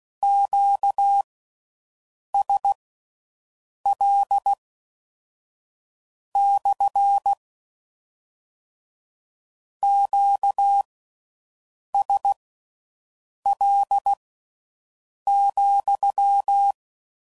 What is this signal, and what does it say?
Morse code, heard as a keyed tone, about 16 wpm, spaced as for 6 wpm, 800 Hz. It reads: QSL / QSL,